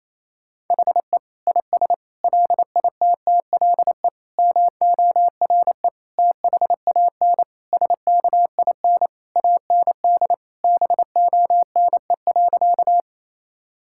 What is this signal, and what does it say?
Morse code, heard as a keyed tone, about 28 wpm, 700 Hz. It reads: HE IS LITTLE MORE THAN SKIN AND BONE.
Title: Sentences from Top 1600 Words 28wpm